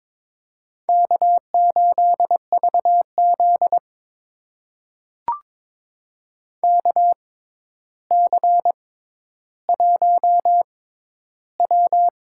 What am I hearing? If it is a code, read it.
Morse code, heard as a keyed tone, about 22 wpm, spaced as for 11 wpm, 700 Hz. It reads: K8VZ KC1W
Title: USA Call Signs - 4 Characters - 6x Character Spacing - 22wpm